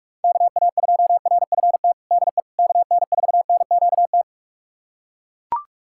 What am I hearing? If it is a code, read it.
Morse code, heard as a keyed tone, about 41 wpm, 700 Hz. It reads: KA2RFT DE KN4NQT
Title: USA Callsign DE Callsign 45wpm